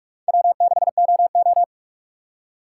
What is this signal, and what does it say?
Morse code, heard as a keyed tone, about 46 wpm, 700 Hz. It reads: W6OO
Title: USA Call Signs - 4 Characters - 3x Character Spacing - 45wpm